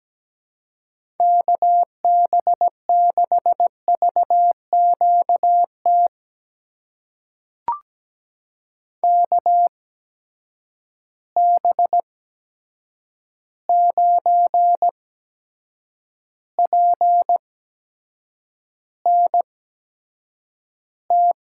Morse code, heard as a keyed tone, about 17 wpm, spaced as for 7 wpm, 700 Hz. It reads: KB6VQT KB9PNT
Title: USA Call Signs - 6 Characters - 8x Character Spacing - 17wpm